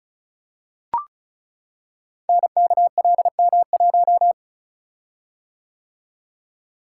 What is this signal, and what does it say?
Morse code, heard as a keyed tone, about 36 wpm, 700 Hz. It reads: NKLM1
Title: Code Groups - 5 Letter:Numbers - 35wpm